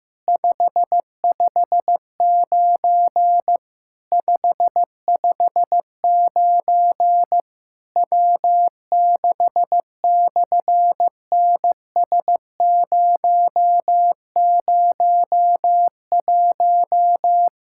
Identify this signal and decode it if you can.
Morse code, heard as a keyed tone, about 15 wpm, 700 Hz. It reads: 559 559 W6/NS001
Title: SOTA QSO Practice 15wpm